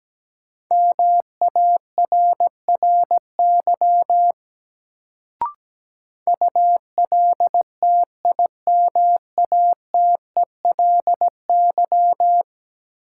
Morse code, heard as a keyed tone, about 17 wpm, 700 Hz. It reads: MARRY ULTIMATELY
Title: Sentences from Top 1200 Words - New Word Drill Alt - 17wpm